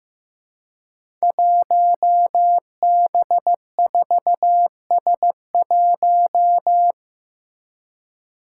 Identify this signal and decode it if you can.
Morse code, heard as a keyed tone, about 15 wpm, 700 Hz. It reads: 1B4S1
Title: Five Challenging Letter-Number Combinations 15wpm